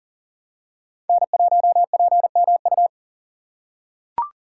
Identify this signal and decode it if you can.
Morse code, heard as a keyed tone, about 40 wpm, 700 Hz. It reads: N1PMU
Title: USA Call Signs - 5 Characters - 6x Character Spacing - 40wpm